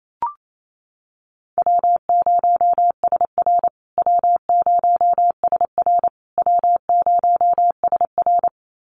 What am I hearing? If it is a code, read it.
Morse code, heard as a keyed tone, about 28 wpm, 700 Hz. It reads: W0SR W0SR W0SR
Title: USA Call Signs Repeated 3x 28wpm